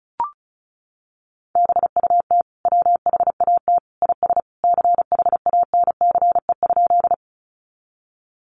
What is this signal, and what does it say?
Morse code, heard as a keyed tone, about 34 wpm, 700 Hz. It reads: BUT WHAT IS CHANCE?
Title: Sentences from Top 800 Words 35wpm